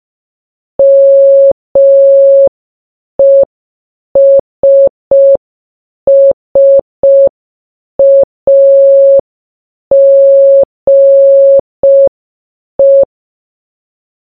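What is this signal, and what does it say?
Morse code, heard as a keyed tone, about 5 wpm, 550 Hz. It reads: MESSAGE
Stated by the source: Morse sound from the game Baba Is You.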